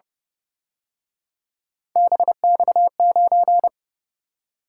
Morse code, heard as a keyed tone, about 29 wpm, 700 Hz. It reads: BX9
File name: Speed-Racing - Three Letter-Number Combinations - 25wpm